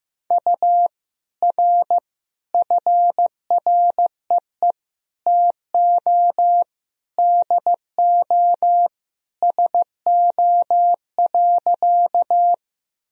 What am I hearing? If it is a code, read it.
Morse code, heard as a keyed tone, about 15 wpm, 700 Hz. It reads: U R FREE TO DO SO.
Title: Sentences from Top 500 Words and Abbreviations 15wpm